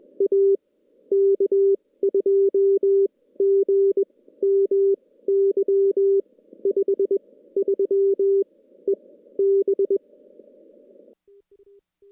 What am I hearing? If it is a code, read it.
Morse code, heard as a keyed tone, about 20 wpm, spaced as for 15 wpm, 390 Hz. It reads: A K2GMY 53 EB
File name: contests / CW-SS-k5tr- -2300